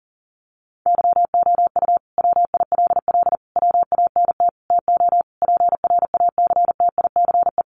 Morse code, heard as a keyed tone, about 40 wpm, 700 Hz. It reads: YOU WILL WANT TO PRACTICE
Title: Sentences from Top 700 Words 40wpm